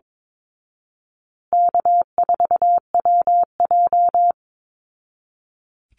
Morse code, heard as a keyed tone, about 22 wpm, 700 Hz. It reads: K4WJ
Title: Easy Call Signs Repeated 2x - 22wpm